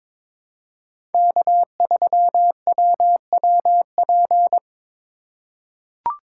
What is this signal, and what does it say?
Morse code, heard as a keyed tone, about 22 wpm, 700 Hz. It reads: K3WWP
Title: K1USN SST QSO Practice - 22wpm